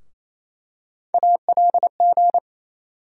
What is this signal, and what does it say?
Morse code, heard as a keyed tone, about 27 wpm, 700 Hz. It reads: ALG
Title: USA Call Sign Postfixes - 3 Letters 28wpm